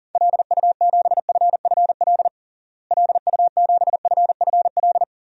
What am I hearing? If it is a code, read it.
Morse code, heard as a keyed tone, about 40 wpm, 700 Hz. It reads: LU7FFL LU7FFL
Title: DXCC-100 Call Signs - 6 Characters - Repeated 3x - 40wpm